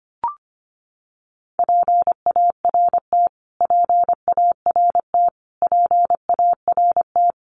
Morse code, heard as a keyed tone, about 25 wpm, 700 Hz. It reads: PART PART PART
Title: Top 200 Words Repeated 25wpm